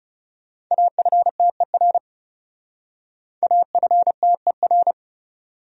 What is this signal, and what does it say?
Morse code, heard as a keyed tone, about 38 wpm, 700 Hz. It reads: AFTER AFTER